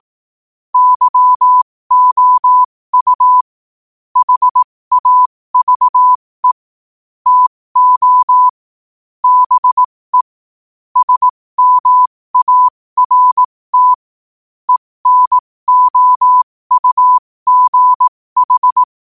Morse code, heard as a keyed tone, about 18 wpm, spaced as for 15 wpm, 1000 Hz. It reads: YOU HAVE TO BE SMART ENOUGH
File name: 15wpm